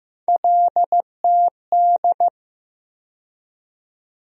Morse code, heard as a keyed tone, about 15 wpm, 700 Hz. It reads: LTD
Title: USA Call Sign Postfixes - 3 Letters 15wpm